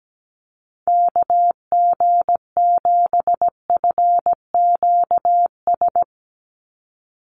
Morse code, heard as a keyed tone, about 17 wpm, 700 Hz. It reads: KG7FQS
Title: USA Call Signs 17wpm